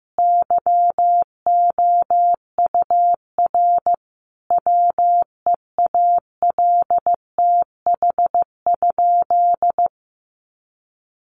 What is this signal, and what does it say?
Morse code, heard as a keyed tone, about 15 wpm, 700 Hz. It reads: YOUR WEALTH?
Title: Sentences from Top 1200 Words 15wpm